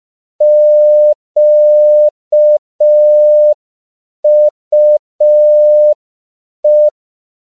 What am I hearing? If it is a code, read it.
Morse code, heard as a keyed tone, about 5 wpm, 600 Hz. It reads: QUE
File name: CW / 5PPM